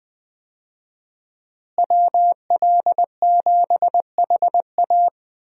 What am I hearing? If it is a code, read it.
Morse code, heard as a keyed tone, about 20 wpm, 700 Hz. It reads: WL7HA
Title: USA Call Signs - 5 Characters - Repeated 2x - 20wpm